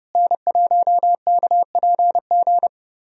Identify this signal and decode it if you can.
Morse code, heard as a keyed tone, about 32 wpm, 700 Hz. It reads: N1KPG
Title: USA Call Signs - 5 Characters - Repeated 2x - 30wpm